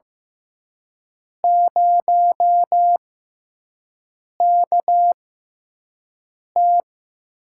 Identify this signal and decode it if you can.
Morse code, heard as a keyed tone, about 15 wpm, spaced as for 5 wpm, 700 Hz. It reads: 0KT
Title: USA Call Signs - 4 Characters - 6x Character Spacing - 15wpm